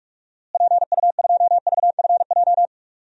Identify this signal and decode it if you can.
Morse code, heard as a keyed tone, about 46 wpm, 700 Hz. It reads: PU2VFJ
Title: DXCC-100 Call Signs - 6 Characters - Repeated 2x - 45wpm